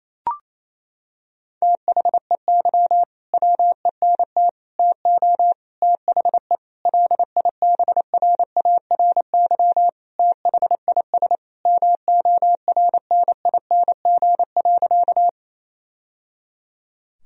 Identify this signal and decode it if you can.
Morse code, heard as a keyed tone, about 28 wpm, 700 Hz. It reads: THEY WENT TO THE LIBRARY THIS MORNING.
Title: Sentences from Top 1000 Words - Review Part 4 - 28wpm